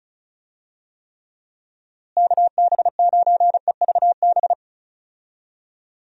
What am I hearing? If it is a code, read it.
Morse code, heard as a keyed tone, about 36 wpm, 700 Hz. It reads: KB9EVB
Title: USA Call Signs - 6 Characters - Repeated 3x - 35wpm